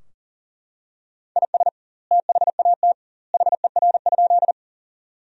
Morse code, heard as a keyed tone, about 40 wpm, 700 Hz. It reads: IS THAT HER?
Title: 2x Word-Spacing - Sentences from Top 100 Words - 40wpm